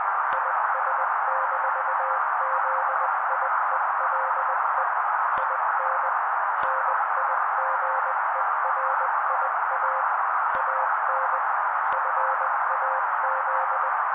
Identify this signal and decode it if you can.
Morse code, heard as a keyed tone, about 20 wpm, 530 Hz. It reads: IS-ZIELE IN NIGERIA AN FAZ
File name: daily / 18wpm noise